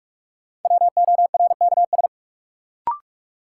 Morse code, heard as a keyed tone, about 45 wpm, 700 Hz. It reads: WORKS
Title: Sentences from Top 1100 Words - New Word Drill Alt - 45wpm